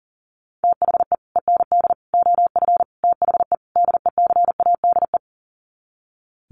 Morse code, heard as a keyed tone, about 40 wpm, 700 Hz. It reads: THE END OF THE DECADE
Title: Sets of 5 Words Encore 40wpm